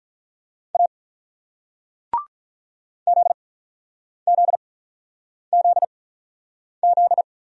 Morse code, heard as a keyed tone, about 47 wpm, 700 Hz. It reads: A Z Z Z Z